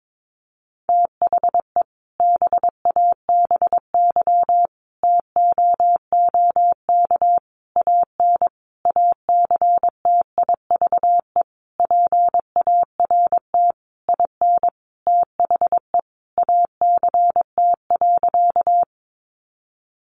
Morse code, heard as a keyed tone, about 22 wpm, 700 Hz. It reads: THE BABY TOOK AN ACTIVE PART IN THE ACT.
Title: Sentences from Top 1300 Words 22wpm